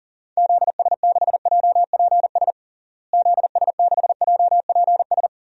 Morse code, heard as a keyed tone, about 40 wpm, 700 Hz. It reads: ZS6JPS ZS6JPS
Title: DXCC-100 Call Signs - 6 Characters - Repeated 3x - 40wpm